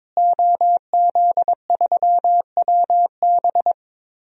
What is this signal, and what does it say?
Morse code, heard as a keyed tone, about 22 wpm, 700 Hz. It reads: OZ3WB
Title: DXCC-100 Call Signs - 5 Characters - Repeated 3x - 22wpm